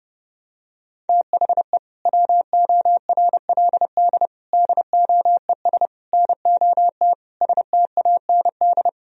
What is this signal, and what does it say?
Morse code, heard as a keyed tone, about 32 wpm, 700 Hz. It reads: THE WORLD DOES NOT STAND
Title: Sentences from Top 1000 Words - Review Part 1 - 30wpm